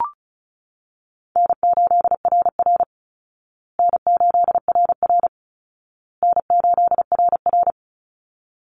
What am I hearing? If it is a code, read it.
Morse code, heard as a keyed tone, about 34 wpm, 700 Hz. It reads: N8RR N8RR N8RR
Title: Easy Call Signs Repeated 3x - 35wpm